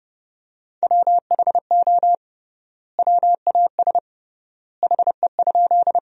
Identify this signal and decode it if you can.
Morse code, heard as a keyed tone, about 30 wpm, 700 Hz. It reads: WHO WAS HE?
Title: 3x Word-Spacing - Sentences from Top 100 Words - 30wpm